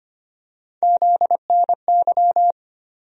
Morse code, heard as a keyed tone, about 25 wpm, 700 Hz. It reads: ZNY
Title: USA Call Sign Postfixes - 3 Letters 25wpm